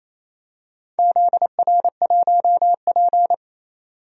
Morse code, heard as a keyed tone, about 27 wpm, 690 Hz. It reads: ZR1P
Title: DXCC-100 Call Signs - 4 Characters - 28wpm